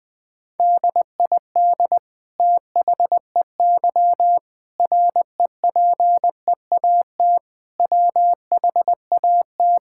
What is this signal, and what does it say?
Morse code, heard as a keyed tone, about 20 wpm, 700 Hz. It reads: DID THEY REPEAT WHAT
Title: Sentences from Top 1600 Words 20wpm